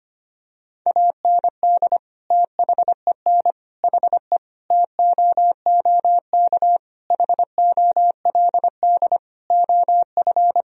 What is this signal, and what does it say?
Morse code, heard as a keyed tone, about 25 wpm, 700 Hz. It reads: AND THEN HE TOOK HOLD OF
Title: Sentences from Top 1000 Words - Review Part 2 - 25wpm